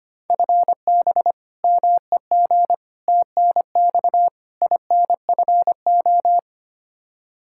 Morse code, heard as a keyed tone, about 25 wpm, 700 Hz. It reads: FB MEG TNX INFO
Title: Practice QSO - Exchange 3 - More Info - 25wpm